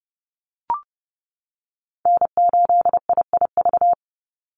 Morse code, heard as a keyed tone, about 30 wpm, 700 Hz. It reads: N8IIV
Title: USA Call Signs 30wpm